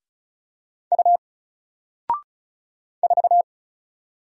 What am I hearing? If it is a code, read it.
Morse code, heard as a keyed tone, about 35 wpm, 700 Hz. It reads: U 4
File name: Single-Letter-Number-35wpm